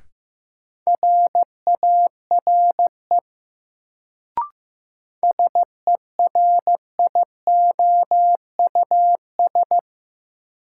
Morse code, heard as a keyed tone, about 16 wpm, 700 Hz. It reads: RARE SERIOUS
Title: Sentences from Top 1100 Words - New Word Drill Alt - 15wpm